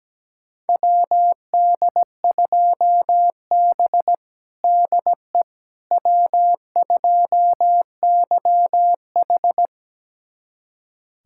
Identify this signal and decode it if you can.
Morse code, heard as a keyed tone, about 17 wpm, 700 Hz. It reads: WD2B DE W2YH
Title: USA Callsign DE Callsign - 4 Characters - 17wpm